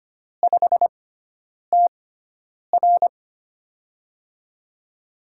Morse code, heard as a keyed tone, about 25 wpm, spaced as for 9 wpm, 700 Hz. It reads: 5TR
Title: USA Call Signs - 6x Character Spacing - 25wpm